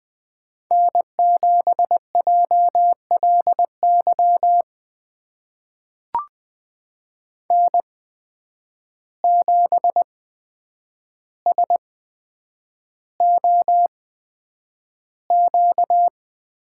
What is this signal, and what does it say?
Morse code, heard as a keyed tone, about 20 wpm, spaced as for 9 wpm, 700 Hz. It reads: N7JLY N7SOQ
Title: USA Call Signs - 5 Characters - 8x Character Spacing - 20wpm